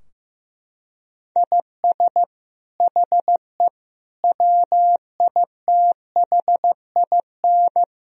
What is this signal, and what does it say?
Morse code, heard as a keyed tone, about 16 wpm, 700 Hz. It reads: IS HE WITHIN
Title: Sentences from Top 1000 Words - Review Part 4 - 15wpm